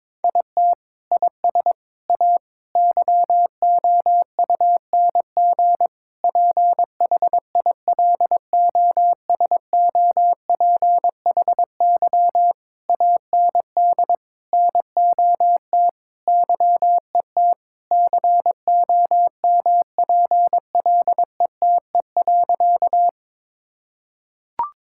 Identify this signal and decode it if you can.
Morse code, heard as a keyed tone, about 22 wpm, 700 Hz. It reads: IT IS A YOUNG PHILOSOPHY AND NOT YET COMPLETE.
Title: Sentences from Top 1000 Words - Review Part 6 - 22wpm